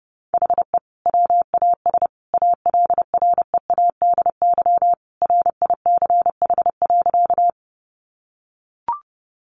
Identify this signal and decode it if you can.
Morse code, heard as a keyed tone, about 30 wpm, 700 Hz. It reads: HE WAS ALREADY RICH.
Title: Sentences from Top 1100 Words 30wpm